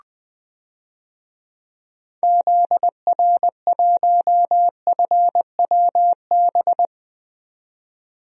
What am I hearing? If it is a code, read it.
Morse code, heard as a keyed tone, about 21 wpm, 700 Hz. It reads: ZR1FWB
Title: DXCC-100 Call Signs - 6 Characters - 20wpm